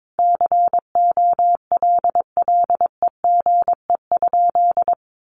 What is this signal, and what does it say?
Morse code, heard as a keyed tone, about 22 wpm, 700 Hz. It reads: COLLEGE?
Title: Sentences from Top 1300 Words 22wpm